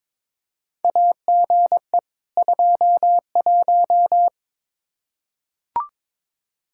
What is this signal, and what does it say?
Morse code, heard as a keyed tone, about 22 wpm, 700 Hz. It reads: AGE 21
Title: Practice QSO - Exchange 3 - More Info - 22wpm